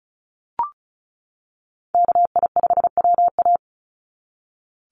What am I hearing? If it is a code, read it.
Morse code, heard as a keyed tone, about 34 wpm, 700 Hz. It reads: KI5WA
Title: NA SOTA Chaser Callsigns 35wpm